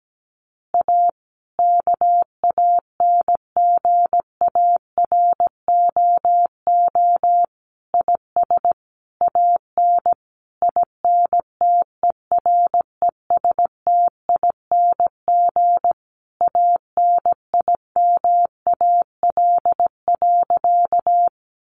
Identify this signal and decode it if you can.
Morse code, heard as a keyed tone, about 17 wpm, 700 Hz. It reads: A KANGAROO IS AN INTERESTING ANIMAL.
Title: Easy Phrases 17wpm